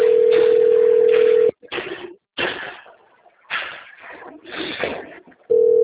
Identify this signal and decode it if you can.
Ringback tone (North America), 440 + 480 Hz.